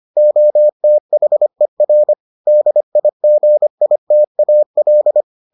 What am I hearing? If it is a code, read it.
Morse code, heard as a keyed tone, about 25 wpm, 600 Hz. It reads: OTHER DIGITAL